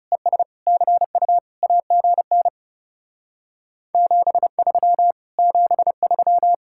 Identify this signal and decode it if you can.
Morse code, heard as a keyed tone, about 37 wpm, 700 Hz. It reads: ES CU AGN 73 73
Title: Speed-Racing Common QSO Phrases - 30wpm